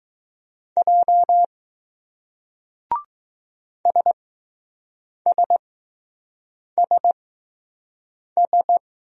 Morse code, heard as a keyed tone, about 20 wpm, spaced as for 6 wpm, 700 Hz. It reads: J SSSS